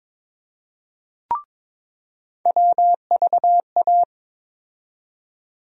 Morse code, heard as a keyed tone, about 22 wpm, 700 Hz. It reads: WVA
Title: USA Call Sign Postfixes - 3 Letters 22wpm